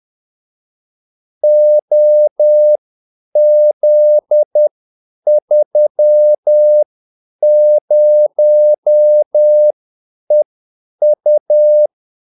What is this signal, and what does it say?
Morse code, heard as a keyed tone, about 10 wpm, spaced as for 8 wpm, 600 Hz. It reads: OZ30EU
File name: Callsigns-10-8wpm